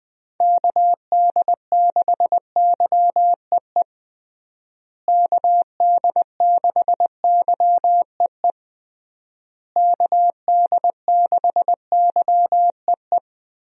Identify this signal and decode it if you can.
Morse code, heard as a keyed tone, about 20 wpm, 700 Hz. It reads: KD6YEE KD6YEE KD6YEE
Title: USA Call Signs - 6 Characters - Repeated 3x - 20wpm